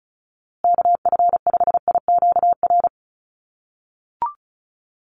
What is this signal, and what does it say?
Morse code, heard as a keyed tone, about 34 wpm, 700 Hz. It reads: KF5IQR
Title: USA Call Signs Repeated 3x 35wpm